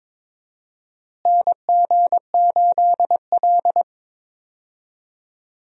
Morse code, heard as a keyed tone, about 22 wpm, 700 Hz. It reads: NG8L
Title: USA Call Signs - 4 Characters - Repeated 3x - 22wpm